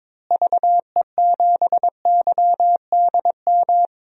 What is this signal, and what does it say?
Morse code, heard as a keyed tone, about 22 wpm, 700 Hz. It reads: VE7YDM
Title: DXCC-100 Call Signs - 6 Characters - Repeated 2x - 22wpm